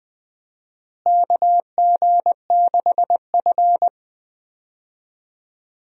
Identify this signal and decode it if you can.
Morse code, heard as a keyed tone, about 21 wpm, 700 Hz. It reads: KG6F
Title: USA Call Signs - 4 Characters - Repeated 2x - 20wpm